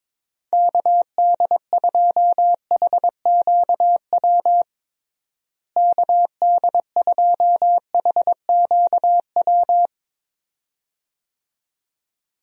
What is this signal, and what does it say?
Morse code, heard as a keyed tone, about 22 wpm, 700 Hz. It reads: KD2HQW KD2HQW
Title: USA Call Signs - 6 Characters - Repeated 2x - 22wpm